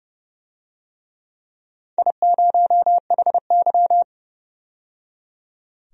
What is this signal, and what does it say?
Morse code, heard as a keyed tone, about 30 wpm, 700 Hz. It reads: I0HY
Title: DXCC-100 Call Signs - 4 Characters - 30wpm